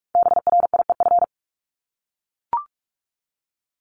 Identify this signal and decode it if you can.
Morse code, heard as a keyed tone, about 44 wpm, 700 Hz. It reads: BRIEF
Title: LY Suffix - Top 100 Words with Base Words 45wpm